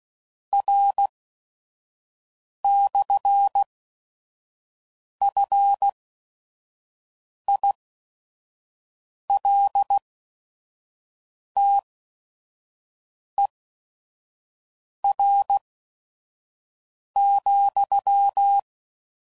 Morse code, heard as a keyed tone, about 16 wpm, spaced as for 5 wpm, 800 Hz. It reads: R/FILTER,
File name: CW Code Groups @5WPM